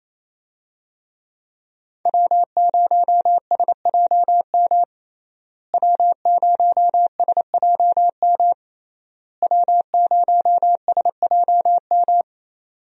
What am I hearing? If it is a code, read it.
Morse code, heard as a keyed tone, about 27 wpm, 700 Hz. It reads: W0SJM W0SJM W0SJM
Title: USA Call Signs - 5 Characters - Repeated 3x - 28wpm